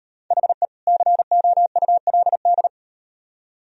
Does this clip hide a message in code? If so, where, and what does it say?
Yes: Morse code from 0.3 to 2.7 s, heard as a keyed tone, about 37 wpm, 700 Hz. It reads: HE COULD